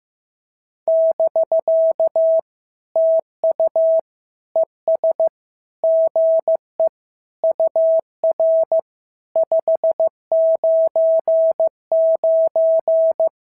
Morse code, heard as a keyed tone, about 14 wpm, 650 Hz. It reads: <BK> TU ES GE UR 599
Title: SOTA QSO Practice 15wpm